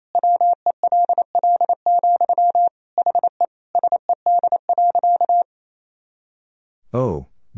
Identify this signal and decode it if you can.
Morse code, heard as a keyed tone, about 28 wpm, 700 Hz. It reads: WELL, HE SED.
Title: Sentences from Top 500 Words and Abbreviations 28wpm